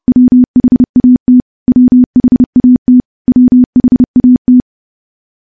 Morse code, heard as a keyed tone, about 30 wpm, 250 Hz. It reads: WHAT WHAT WHAT
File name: morse / morse-training-30WPM